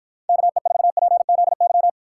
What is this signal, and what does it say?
Morse code, heard as a keyed tone, about 52 wpm, 700 Hz. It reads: KE4PZY